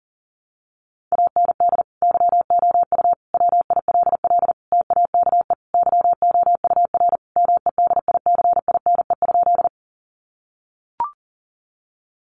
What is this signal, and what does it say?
Morse code, heard as a keyed tone, about 40 wpm, 700 Hz. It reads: AND YOU WILL TAKE YOUR MEDICINE?
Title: Sentences from Top 900 Words 40wpm